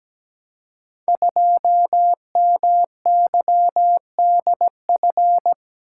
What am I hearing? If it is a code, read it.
Morse code, heard as a keyed tone, about 17 wpm, 700 Hz. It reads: 2MYDF
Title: Code Groups - 5 Letter:Numbers - 17wpm